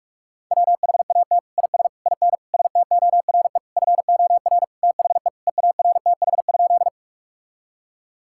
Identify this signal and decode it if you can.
Morse code, heard as a keyed tone, about 46 wpm, 700 Hz. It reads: WHAT IS IN STORE FOR THE EARTH?
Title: Sentences from Top 1000 Words - Review Part 5 - 45wpm